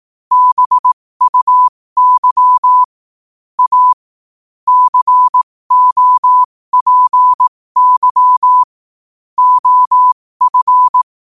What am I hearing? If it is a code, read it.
Morse code, heard as a keyed tone, about 18 wpm, spaced as for 15 wpm, 1000 Hz. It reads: BUY A COPY OF
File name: qotdfiles / 15wpm